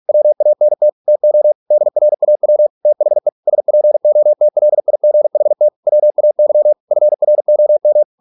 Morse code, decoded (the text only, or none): WANT TO DRAW THE SPOTLIGHT WAY FROM